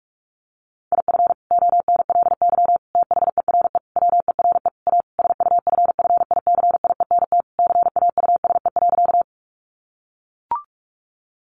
Morse code, heard as a keyed tone, about 44 wpm, 700 Hz. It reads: IF ONLY THERE WERE A SUFFICIENT CAUSE.
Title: Sentences from Top 1400 Words 45wpm